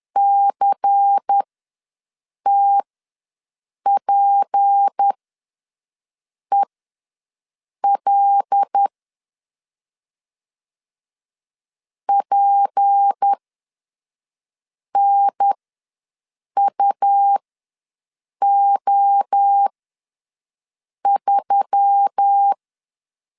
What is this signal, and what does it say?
Morse code, heard as a keyed tone, about 10 wpm, spaced as for 5 wpm, 800 Hz. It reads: CTPEL PNUO3
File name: LEZIONI / 6 PKX0 / 6 P ws 10wpm 3751